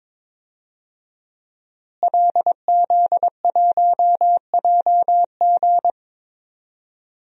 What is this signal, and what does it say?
Morse code, heard as a keyed tone, about 22 wpm, 700 Hz. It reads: LZ1JG
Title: DXCC-100 Call Signs - 5 Characters - Repeated 2x - 22wpm